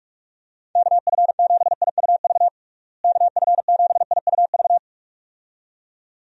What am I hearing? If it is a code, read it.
Morse code, heard as a keyed tone, about 46 wpm, 700 Hz. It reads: KF7IUV KF7IUV
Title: USA Call Signs - 6 Characters - Repeated 2x - 45wpm